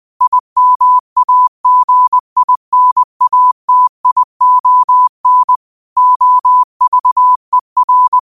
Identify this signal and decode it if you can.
Morse code, heard as a keyed tone, about 20 wpm, 1000 Hz. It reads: IMAGINATION OVER
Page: Quotes for Fri, 15 Aug 2025 in Morse Code at 20 words per minute.